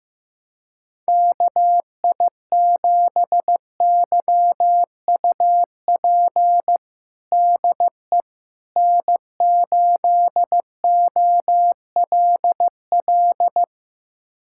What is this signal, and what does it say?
Morse code, heard as a keyed tone, about 15 wpm, 700 Hz. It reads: KI7YUP DE N8OLL
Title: USA Callsign DE Callsign 15wpm